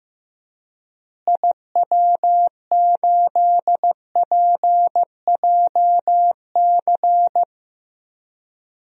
Morse code, heard as a keyed tone, about 16 wpm, 700 Hz. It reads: IW8PJC
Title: DXCC-100 Call Signs - 6 Characters - Repeated 2x - 15wpm